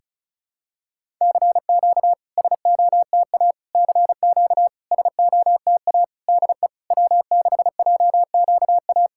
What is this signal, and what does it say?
Morse code, heard as a keyed tone, about 36 wpm, 690 Hz. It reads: CQ SOTA CQ SOTA DE W6JQA
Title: SOTA QSO Practice 35wpm